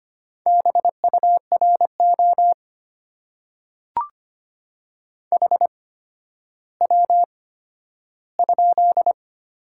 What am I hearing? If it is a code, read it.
Morse code, heard as a keyed tone, about 25 wpm, spaced as for 12 wpm, 700 Hz. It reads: BURO HW?
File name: Top 90 QSO Elements - 8x Character Spacing - 25wpm